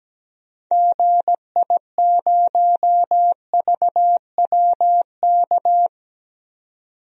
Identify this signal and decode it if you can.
Morse code, heard as a keyed tone, about 17 wpm, 700 Hz. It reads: GI0VWK
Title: DXCC-100 Call Signs - 6 Characters - 17wpm